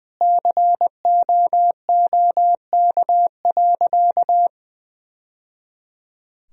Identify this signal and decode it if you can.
Morse code, heard as a keyed tone, about 21 wpm, 700 Hz. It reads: COOK.
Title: Sentences from Top 1000 Words - Review Part 4 - 20wpm